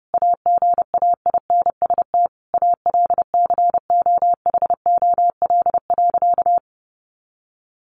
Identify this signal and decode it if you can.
Morse code, heard as a keyed tone, about 30 wpm, 700 Hz. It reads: AGAINST ALCOHOL.
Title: Sentences from Top 1400 Words 30wpm